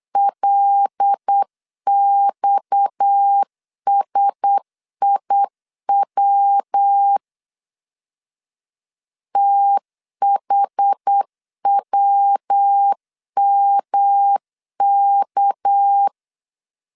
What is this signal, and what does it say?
Morse code, heard as a keyed tone, about 8 wpm, 800 Hz. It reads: LXSIW THWMK
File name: CW / DET / 1 6 1500C S B 8wpm 5142